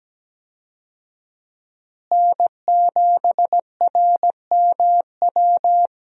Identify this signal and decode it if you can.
Morse code, heard as a keyed tone, about 17 wpm, 700 Hz. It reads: N7RMW
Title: USA Call Signs - 5 Characters - Repeated 3x - 17wpm